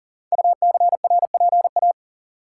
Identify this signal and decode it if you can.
Morse code, heard as a keyed tone, about 39 wpm, 700 Hz. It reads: UCRPA
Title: Code Groups - 5 Letter:Numbers - 40wpm